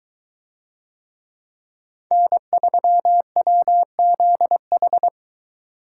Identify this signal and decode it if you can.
Morse code, heard as a keyed tone, about 22 wpm, 700 Hz. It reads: N3WZH